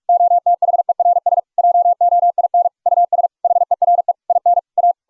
Morse code, heard as a keyed tone, about 41 wpm, 700 Hz. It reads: OTHERS JOIN US HERE IN A